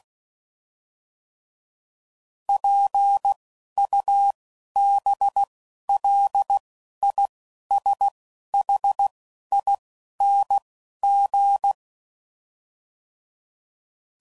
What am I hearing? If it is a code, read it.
Morse code, heard as a keyed tone, about 16 wpm, 800 Hz. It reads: P U B L I S H I N G